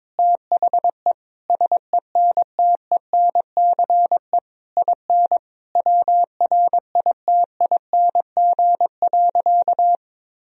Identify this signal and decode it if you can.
Morse code, heard as a keyed tone, about 22 wpm, 700 Hz. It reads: THE SENTENCE IN WRITING.
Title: Sentences from Top 1000 Words - Review Part 1 - 22wpm